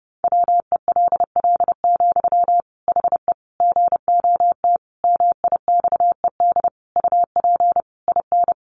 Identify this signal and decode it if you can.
Morse code, heard as a keyed tone, about 30 wpm, 700 Hz. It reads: WELL, HE GOT MIXED UP IN
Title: Sentences from Top 1500 Words 30wpm